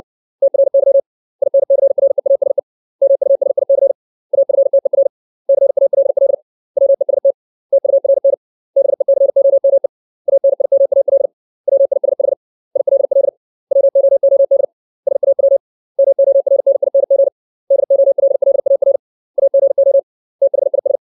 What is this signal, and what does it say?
Morse code, heard as a keyed tone, about 60 wpm, 550 Hz. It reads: TRY INCREASE MUSIC AFTER CALL WEST TURN BECOME ANIMAL WISH ILL WOOD SAW MORNING DOLLAR AGO THIS
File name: 632 words / 632 Words 60WPM